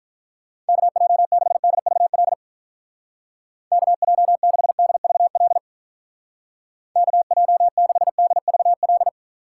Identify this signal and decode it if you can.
Morse code, heard as a keyed tone, about 46 wpm, 700 Hz. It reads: KJ6DVL KJ6DVL KJ6DVL